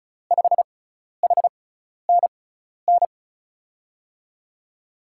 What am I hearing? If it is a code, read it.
Morse code, heard as a keyed tone, about 36 wpm, spaced as for 12 wpm, 700 Hz. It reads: 5HNN
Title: USA Call Signs - 6x Character Spacing - 35wpm